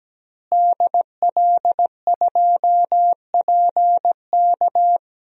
Morse code, heard as a keyed tone, about 17 wpm, 700 Hz. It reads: DL2PK
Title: DXCC-100 Call Signs - 5 Characters - Repeated 3x - 17wpm